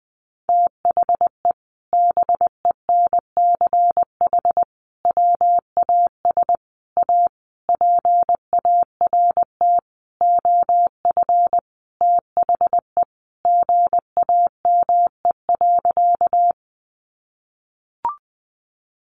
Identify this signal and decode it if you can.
Morse code, heard as a keyed tone, about 20 wpm, 700 Hz. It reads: THE BENCH WAS A PART OF THE GAME.
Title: Sentences from Top 1400 Words 20wpm